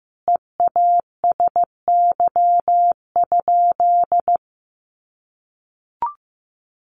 Morse code, heard as a keyed tone, about 14 wpm, 700 Hz. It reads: EASY?
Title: Sentences from Top 1500 Words 15wpm